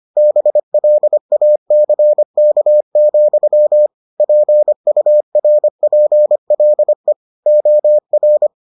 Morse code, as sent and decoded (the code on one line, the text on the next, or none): -... .-.. .- -.-. -.- --..-- / .--. ..- .-. .--. .-.. / --- .-.
BLACK, PURPLE OR